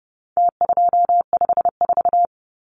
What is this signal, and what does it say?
Morse code, heard as a keyed tone, about 30 wpm, 700 Hz. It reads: T254
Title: Numbers - Contest Style - 4-Digit 30wpm